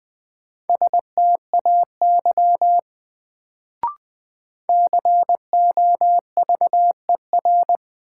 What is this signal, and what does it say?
Morse code, heard as a keyed tone, about 21 wpm, 700 Hz. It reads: STAY COVER
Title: Sentences from Top 500 Words - New Word Drill - 20wpm